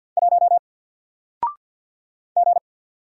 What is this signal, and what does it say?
Morse code, heard as a keyed tone, about 48 wpm, 700 Hz. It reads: 1 G
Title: Single Letter-Number 50wpm